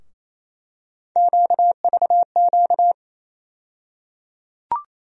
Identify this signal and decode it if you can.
Morse code, heard as a keyed tone, about 28 wpm, 700 Hz. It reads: QVQ
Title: USA Call Sign Postfixes - 3 Letters 28wpm